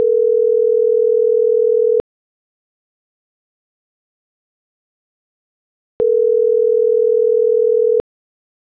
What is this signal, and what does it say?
Ringback tone (North America), 440 + 480 Hz.